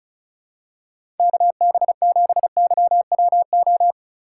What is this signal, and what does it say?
Morse code, heard as a keyed tone, about 36 wpm, 700 Hz. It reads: KB7YWO
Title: USA Call Signs - 6 Characters - 4x Character Spacing - 35wpm